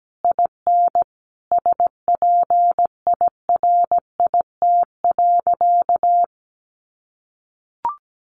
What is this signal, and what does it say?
Morse code, heard as a keyed tone, about 17 wpm, 700 Hz. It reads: IN SPIRIT.
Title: Sentences from Top 1400 Words 17wpm